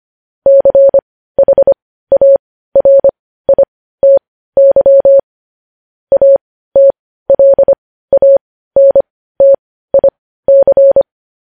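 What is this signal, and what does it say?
Morse code, heard as a keyed tone, about 25 wpm, spaced as for 15 wpm, 550 Hz. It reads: CHARITY ATLANTIC
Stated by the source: Morse code news headlines at 15 WPM.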